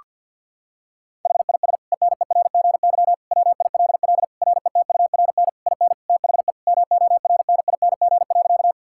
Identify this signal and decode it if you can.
Morse code, heard as a keyed tone, about 52 wpm, 700 Hz. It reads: HIS ENERGY WILL RETURN IN THE MORNING.
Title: Sentences from Top 1000 Words - Review Part 6 - 50wpm